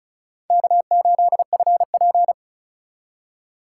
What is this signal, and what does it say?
Morse code, heard as a keyed tone, about 34 wpm, 700 Hz. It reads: K8FP
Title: USA Call Signs - 3x Character Spacing - 35wpm